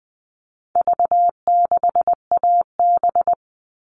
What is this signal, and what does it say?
Morse code, heard as a keyed tone, about 20 wpm, 700 Hz. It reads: V6AB
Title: All Cut Numbers - 4-Digit 20wpm